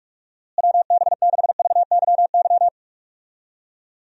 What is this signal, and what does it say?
Morse code, heard as a keyed tone, about 46 wpm, 700 Hz. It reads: WB6VYY
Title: USA Call Signs - 6 Characters - Repeated 2x - 45wpm